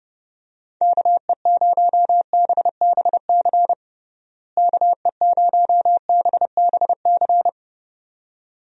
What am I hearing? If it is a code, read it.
Morse code, heard as a keyed tone, about 30 wpm, 700 Hz. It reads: KE0BBC KE0BBC
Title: USA Call Signs - 6 Characters - Repeated 2x - 30wpm